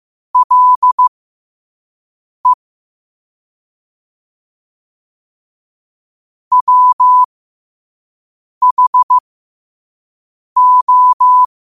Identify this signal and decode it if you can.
Morse code, heard as a keyed tone, about 15 wpm, spaced as for 5 wpm, 1000 Hz. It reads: LE WHO
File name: qotdfiles / 5wpm